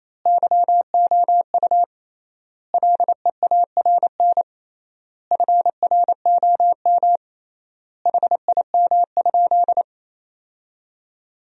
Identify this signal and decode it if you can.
Morse code, heard as a keyed tone, about 27 wpm, 700 Hz. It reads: YOU LEARN FROM HIM?
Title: 3x Word-Spacing - Sentences from Top 300 Words - 28wpm